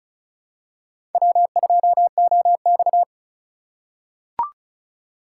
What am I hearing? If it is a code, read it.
Morse code, heard as a keyed tone, about 35 wpm, 700 Hz. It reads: W2OX
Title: USA Call Signs - 4 Characters - 4x Character Spacing - 35wpm